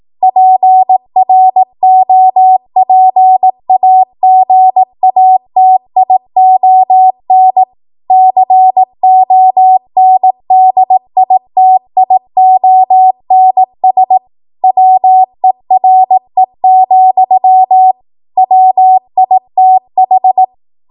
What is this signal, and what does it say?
Morse code, heard as a keyed tone, about 18 wpm, 750 Hz. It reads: PROPAGATION CONDITIONS WERE, WITH
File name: Archive / 18 WPM / 18WPM